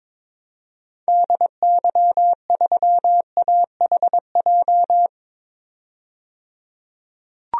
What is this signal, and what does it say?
Morse code, heard as a keyed tone, about 21 wpm, 700 Hz. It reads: DY3AHJ
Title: DXCC-100 Call Signs - 6 Characters - 22wpm